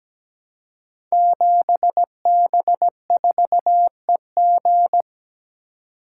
Morse code, heard as a keyed tone, about 17 wpm, 700 Hz. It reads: 7B4EG
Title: Code Groups - 5 Letter:Numbers - 17wpm